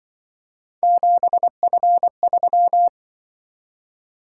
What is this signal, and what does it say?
Morse code, heard as a keyed tone, about 24 wpm, 700 Hz. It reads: 7F3
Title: Speed-Racing - Three Letter-Number Combinations - 20wpm